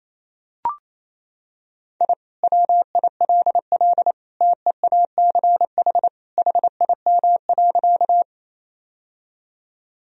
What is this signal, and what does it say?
Morse code, heard as a keyed tone, about 28 wpm, 700 Hz. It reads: I WILL TEACH HIM.
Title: Sentences from Top 1000 Words - Review Part 2 - 28wpm